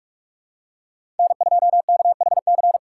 Morse code, heard as a keyed tone, about 46 wpm, 700 Hz. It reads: N2KHC
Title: USA Call Signs - 2x Character Spacing - 45wpm